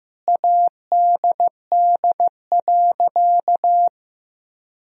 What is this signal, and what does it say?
Morse code, heard as a keyed tone, about 15 wpm, 700 Hz. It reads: ADD.
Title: Sentences from Top 1000 Words - Review Part 1 - 15wpm